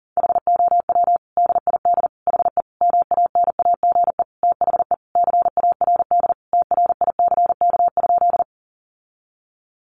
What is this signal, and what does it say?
Morse code, heard as a keyed tone, about 40 wpm, 700 Hz. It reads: HOW DID HE MANAGE THE CARD TRICK?
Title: Sentences from Top 1500 Words 40wpm